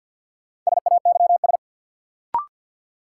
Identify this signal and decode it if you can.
Morse code, heard as a keyed tone, about 53 wpm, 700 Hz. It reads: SAYS
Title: Sets of 2 Words - New Word Drill - 50wpm